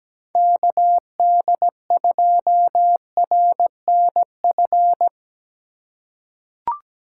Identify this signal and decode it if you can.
Morse code, heard as a keyed tone, about 17 wpm, 700 Hz. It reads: KD2RNF
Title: USA Call Signs - 6 Characters - 3x Character Spacing - 17wpm